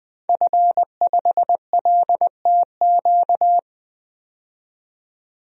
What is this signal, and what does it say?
Morse code, heard as a keyed tone, about 20 wpm, 700 Hz. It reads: F5LTQ
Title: DXCC-100 Call Signs - 5 Characters - 20wpm